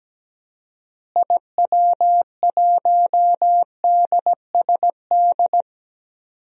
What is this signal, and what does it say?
Morse code, heard as a keyed tone, about 17 wpm, 700 Hz. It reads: IW1DSD
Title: DXCC-100 Call Signs - 6 Characters - Repeated 2x - 17wpm